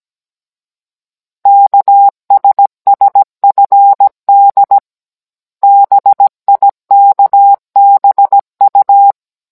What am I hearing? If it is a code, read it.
Morse code, heard as a keyed tone, about 17 wpm, 800 Hz. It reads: KSSFD BIKBU
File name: Cours-NR19-16mm